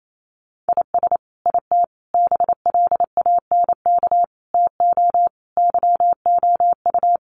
Morse code, heard as a keyed tone, about 28 wpm, 700 Hz. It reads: IS IT BLANK TO YOU
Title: Sentences from Top 1500 Words 28wpm